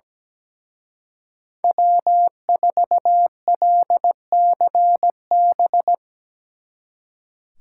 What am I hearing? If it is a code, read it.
Morse code, heard as a keyed tone, about 18 wpm, 700 Hz. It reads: W4LCB
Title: Speed Racing - USA 1x3 Call Signs 17wpm